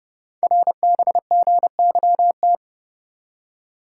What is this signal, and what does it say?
Morse code, heard as a keyed tone, about 29 wpm, 700 Hz. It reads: RBGYT
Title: Code Groups - 5 Letters - 30wpm